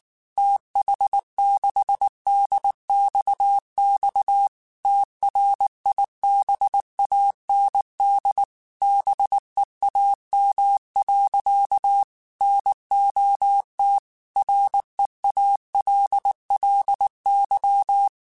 Morse code, heard as a keyed tone, about 19 wpm, 800 Hz. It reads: TH6DXX TRIBAND BEAM. NOT REALLY
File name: QSOs / QSOs-mp3s / WD0BGP 1@18